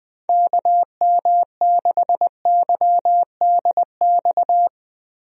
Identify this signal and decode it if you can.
Morse code, heard as a keyed tone, about 21 wpm, 700 Hz. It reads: KM6YDX
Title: USA Call Signs - 6 Characters - Repeated 3x - 20wpm